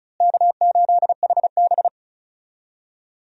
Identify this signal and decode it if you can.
Morse code, heard as a keyed tone, about 36 wpm, 700 Hz. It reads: K8HB
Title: USA Call Signs - 4 Characters - Repeated 2x - 35wpm